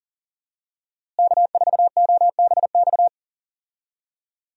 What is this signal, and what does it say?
Morse code, heard as a keyed tone, about 40 wpm, 700 Hz. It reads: K4OBX
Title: USA Call Signs - 5 Characters - Repeated 2x - 40wpm